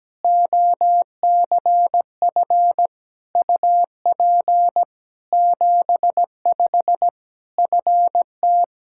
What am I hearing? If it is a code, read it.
Morse code, heard as a keyed tone, about 17 wpm, 700 Hz. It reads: OCF UP 75 FT
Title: Practice QSO - Exchange 2 - Info - 17wpm